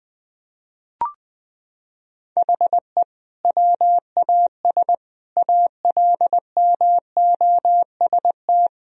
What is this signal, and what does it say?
Morse code, heard as a keyed tone, about 20 wpm, 700 Hz. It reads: HE WAS ALMOST
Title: Sentences from Top 1000 Words - Review Part 4 - 20wpm